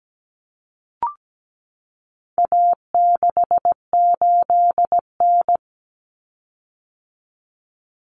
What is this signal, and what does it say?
Morse code, heard as a keyed tone, about 17 wpm, 700 Hz. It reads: A68N
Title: Common Cut Numbers - 4-Digit 17wpm